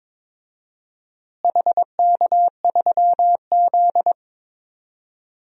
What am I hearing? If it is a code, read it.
Morse code, heard as a keyed tone, about 22 wpm, 700 Hz. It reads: HK3Z
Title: DXCC-100 Call Signs - 4 Characters - Repeated 2x - 22wpm